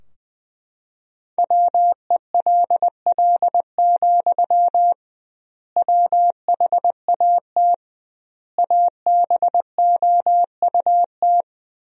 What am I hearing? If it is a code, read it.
Morse code, heard as a keyed tone, about 19 wpm, 700 Hz. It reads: WELL, WHAT ABOUT
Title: 2x Word-Spacing - Sentences from Top 500 Words - 20wpm